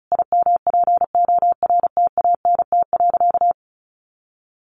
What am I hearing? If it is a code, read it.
Morse code, heard as a keyed tone, about 34 wpm, 700 Hz. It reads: IMPORTANT.
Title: Sentences from Top 600 Words 35wpm